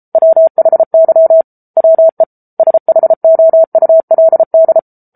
Morse code, heard as a keyed tone, about 34 wpm, 650 Hz. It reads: WHY WE SHOULD
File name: Book Two / 07-Chapter 7 - 33 WPM